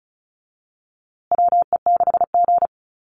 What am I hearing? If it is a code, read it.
Morse code, heard as a keyed tone, about 34 wpm, 700 Hz. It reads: WE6G
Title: USA Call Signs Repeated 3x 35wpm